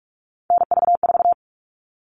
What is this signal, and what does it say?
Morse code, heard as a keyed tone, about 44 wpm, 700 Hz. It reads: NV4
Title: All Cut Numbers - 3-Digit 45wpm